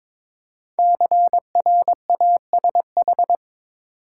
Morse code, heard as a keyed tone, about 22 wpm, 700 Hz. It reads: CRASH
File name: 5 Letter Words - First 3 Letters Given - 22wpm